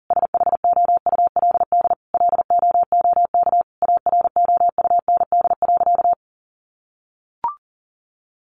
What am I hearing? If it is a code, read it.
Morse code, heard as a keyed tone, about 40 wpm, 700 Hz. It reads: SHOULD LOOK AROUND.
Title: Sentences from Top 800 Words 40wpm